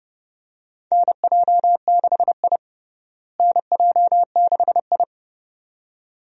Morse code, heard as a keyed tone, about 30 wpm, 700 Hz. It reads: NJ6I NJ6I
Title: USA Call Signs - 4 Characters - Repeated 2x - 30wpm